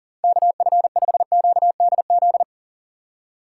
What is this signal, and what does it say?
Morse code, heard as a keyed tone, about 37 wpm, 700 Hz. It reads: KF5QDZ
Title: USA Call Signs - 6 Characters - 6x Character Spacing - 40wpm